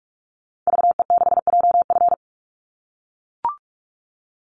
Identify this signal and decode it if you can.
Morse code, heard as a keyed tone, about 44 wpm, 700 Hz. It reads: VE6JF
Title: Call Signs - 45wpm